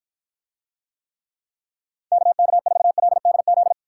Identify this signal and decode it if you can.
Morse code, heard as a keyed tone, about 49 wpm, 700 Hz. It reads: KK4LDZ